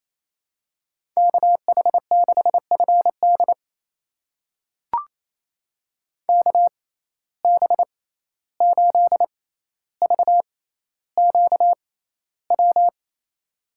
Morse code, heard as a keyed tone, about 28 wpm, spaced as for 14 wpm, 700 Hz. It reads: KH6FD KB8VQW
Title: USA Call Signs - 6x Character Spacing - 28wpm